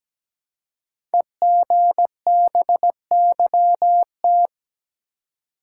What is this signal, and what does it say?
Morse code, heard as a keyed tone, about 17 wpm, 700 Hz. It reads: EGBYT
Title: Code Groups - 5 Letter:Numbers - 17wpm